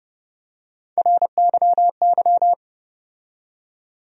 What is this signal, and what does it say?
Morse code, heard as a keyed tone, about 32 wpm, 700 Hz. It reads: RYY
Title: USA Call Sign Postfixes - 3 Letters 30wpm